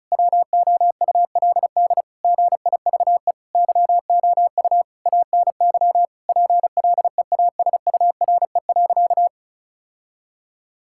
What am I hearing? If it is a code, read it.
Morse code, heard as a keyed tone, about 36 wpm, 700 Hz. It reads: WOULD GIVE YOU ANY PLEASURE.
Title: Sentences from Top 1600 Words 35wpm